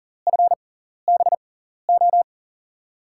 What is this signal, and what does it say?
Morse code, heard as a keyed tone, about 38 wpm, spaced as for 14 wpm, 700 Hz. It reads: FBO
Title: USA Call Signs - 5 Characters - 6x Character Spacing - 40wpm